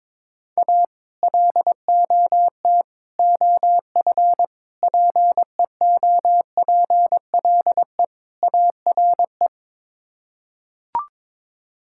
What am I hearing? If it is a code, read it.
Morse code, heard as a keyed tone, about 22 wpm, 700 Hz. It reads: A LOT OF PEOPLE ARE
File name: practice / 22wpm / Build A Phrase - 5-Word Phrases - 22wpm